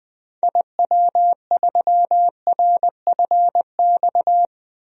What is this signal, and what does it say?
Morse code, heard as a keyed tone, about 20 wpm, 700 Hz. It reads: IW3RFX
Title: DXCC-100 Call Signs - 6 Characters - Repeated 2x - 20wpm